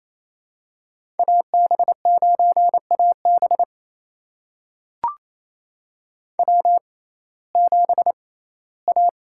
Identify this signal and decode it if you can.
Morse code, heard as a keyed tone, about 28 wpm, spaced as for 17 wpm, 700 Hz. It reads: AB9AB W7A
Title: USA Call Signs - 5 Characters - 6x Character Spacing - 28wpm